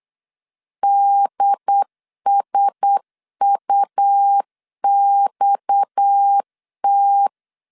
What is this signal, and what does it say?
Morse code, heard as a keyed tone, about 8 wpm, 800 Hz. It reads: DSUXT
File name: LEZIONI / 6 PKX0 / 6 PKX 8wpm 3041